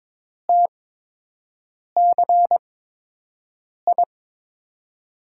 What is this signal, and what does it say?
Morse code, heard as a keyed tone, about 22 wpm, spaced as for 6 wpm, 700 Hz. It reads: TCI
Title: USA Call Signs - 8x Character Spacing - 22wpm